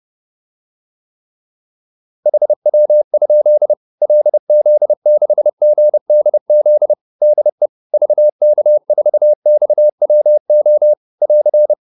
Morse code, heard as a keyed tone, about 30 wpm, 600 Hz. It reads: HW? LZ6GDZ DE VK4XWO <AR>